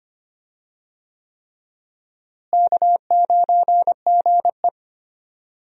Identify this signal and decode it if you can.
Morse code, heard as a keyed tone, about 25 wpm, 700 Hz. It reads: K9GE
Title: USA Call Signs - 4 Characters - Repeated 3x - 25wpm